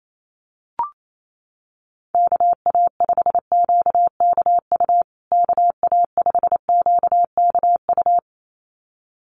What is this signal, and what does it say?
Morse code, heard as a keyed tone, about 28 wpm, 700 Hz. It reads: KA5QKU KA5QKU
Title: USA Call Signs Repeated 2x 28wpm